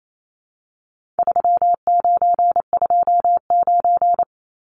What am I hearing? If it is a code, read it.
Morse code, heard as a keyed tone, about 28 wpm, 700 Hz. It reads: 3929
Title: Numbers - 4-Digit 28wpm